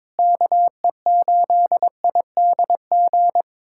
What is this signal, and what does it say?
Morse code, heard as a keyed tone, about 22 wpm, 700 Hz. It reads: KE8IDG
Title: USA Callsign DE Callsign 22wpm